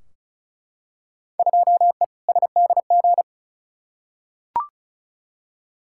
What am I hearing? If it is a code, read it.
Morse code, heard as a keyed tone, about 36 wpm, 700 Hz. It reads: 2E SDG
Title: Field Day Exchanges - 35wpm